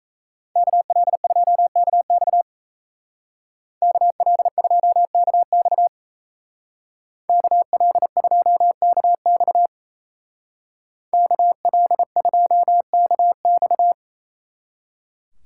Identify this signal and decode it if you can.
Morse code, heard as a keyed tone, about 33 wpm, 700 Hz. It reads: KL2KX KL2KX KL2KX KL2KX
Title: Speed Racing - USA 2x2 Call Signs 28wpm